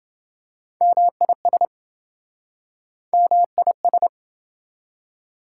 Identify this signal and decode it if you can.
Morse code, heard as a keyed tone, about 28 wpm, 700 Hz. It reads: MIS MIS